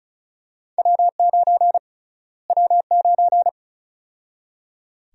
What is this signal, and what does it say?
Morse code, heard as a keyed tone, about 34 wpm, 700 Hz. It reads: W9 W9
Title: USA Call Sign Prefixes - Repeated 2x - 35wpm